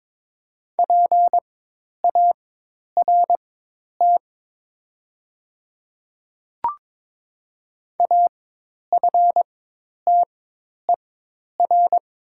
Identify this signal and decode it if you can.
Morse code, heard as a keyed tone, about 23 wpm, spaced as for 10 wpm, 700 Hz. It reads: PART AFTER
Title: 4x Character Spacing - Top 200 Words - 22wpm